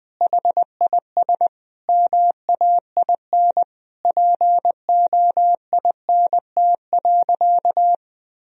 Morse code, heard as a keyed tone, about 20 wpm, 700 Hz. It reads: HIS MAIN POINT.
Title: Sentences from Top 1000 Words - Review Part 2 - 20wpm